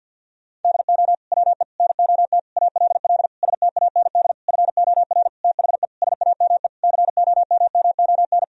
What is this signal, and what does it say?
Morse code, heard as a keyed tone, about 47 wpm, 700 Hz. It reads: DO WE NOT ALL STAND FOR THE SAME COMMON
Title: Sentences from Top 1000 Words - Review Part 3 - 50wpm